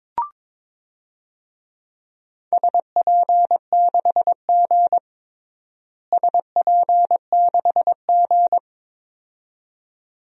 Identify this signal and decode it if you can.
Morse code, heard as a keyed tone, about 22 wpm, 700 Hz. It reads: SP6G SP6G
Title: DXCC-100 Call Signs - 4 Characters - Repeated 2x - 22wpm